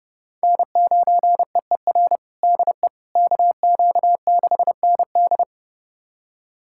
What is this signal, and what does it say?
Morse code, heard as a keyed tone, about 30 wpm, 700 Hz. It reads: N9EER DE KQ6ND
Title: USA Callsign DE Callsign - 5 Characters - 30wpm